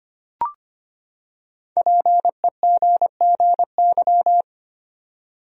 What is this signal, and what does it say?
Morse code, heard as a keyed tone, about 25 wpm, 700 Hz. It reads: PEGGY
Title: USA Common Names - 25wpm